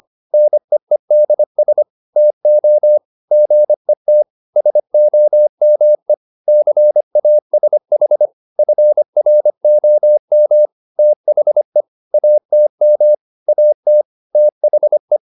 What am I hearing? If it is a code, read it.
Morse code, heard as a keyed tone, about 25 wpm, 600 Hz. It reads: NEEDS TO GET SOME CASH FROM THE ATM AT THE